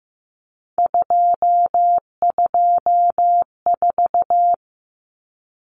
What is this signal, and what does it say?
Morse code, heard as a keyed tone, about 15 wpm, 700 Hz. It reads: 224
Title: Common Cut Numbers - 3-Digit 15wpm